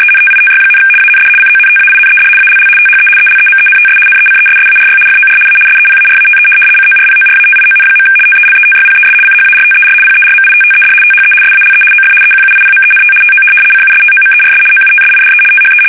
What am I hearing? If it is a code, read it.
RTTY, about 75 baud. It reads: ENTITY, AND CONTAINED ALL THE ELEMENTS NECESSARY FOR A SEPARATE CORPORATION. TELETYPES CHARTER PERMITTED THE SALE OF EQUIPMENT TO CUSTOMERS OUTSIDE THE
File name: fsk850-75B / teletype-corp